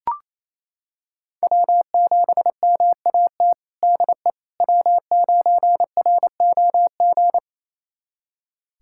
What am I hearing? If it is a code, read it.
Morse code, heard as a keyed tone, about 28 wpm, 700 Hz. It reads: W7MAT DE W9ROG
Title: USA Callsign DE Callsign - 5 Characters - 28wpm